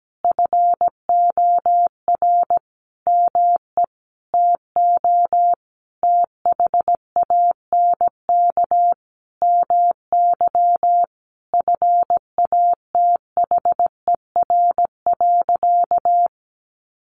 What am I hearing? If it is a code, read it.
Morse code, heard as a keyed tone, about 17 wpm, 700 Hz. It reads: FOR ME TO THANK MY FATHER.
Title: Sentences from Top 1100 Words 17wpm